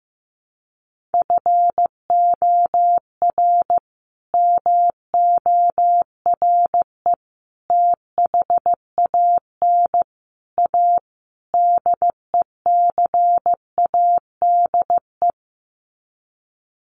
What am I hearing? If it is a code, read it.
Morse code, heard as a keyed tone, about 15 wpm, 700 Hz. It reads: FOR MORE THAN A DECADE
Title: Sets of 5 Words 15wpm